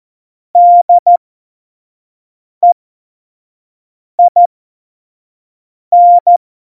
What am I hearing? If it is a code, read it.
Morse code, heard as a keyed tone, about 14 wpm, spaced as for 5 wpm, 700 Hz. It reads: DEIN
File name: Tempo 25 F70